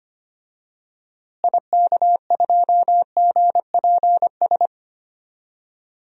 Morse code, heard as a keyed tone, about 25 wpm, 700 Hz. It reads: IK2GPS
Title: DXCC-100 Call Signs - 6 Characters - Repeated 2x - 25wpm